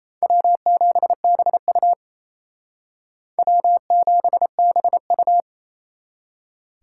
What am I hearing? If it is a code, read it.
Morse code, heard as a keyed tone, about 32 wpm, 700 Hz. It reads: W7BU W7BU
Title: Speed Racing - USA 1x2 Call Signs 28wpm